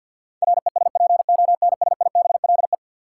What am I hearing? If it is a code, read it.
Morse code, heard as a keyed tone, about 49 wpm, 700 Hz. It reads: RESPONSIBLE
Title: Sentences from Top 1000 Words - Review Part 7 - New Word Drill - 50wpm